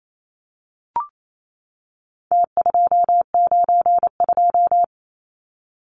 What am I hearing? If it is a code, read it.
Morse code, heard as a keyed tone, about 28 wpm, 700 Hz. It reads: T292
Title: Numbers - Contest Style - 4-Digit 28wpm